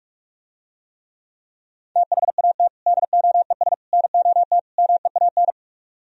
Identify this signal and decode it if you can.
Morse code, heard as a keyed tone, about 44 wpm, 690 Hz. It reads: THAT DOES NOT MEAN